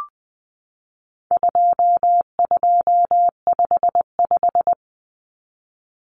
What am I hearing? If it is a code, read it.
Morse code, heard as a keyed tone, about 20 wpm, 700 Hz. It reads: 2255
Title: Numbers - 4-Digit 20wpm